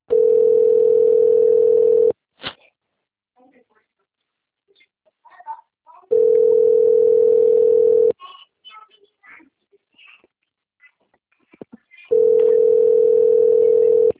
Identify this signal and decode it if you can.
Ringback tone (North America), 440 + 480 Hz, with speech.
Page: Cosmopolis - SP